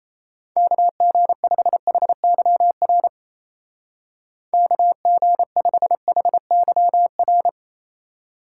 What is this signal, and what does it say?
Morse code, heard as a keyed tone, about 35 wpm, 700 Hz. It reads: KG5HYR KG5HYR
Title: Speed Racing - USA 2x3 Call Signs 28wpm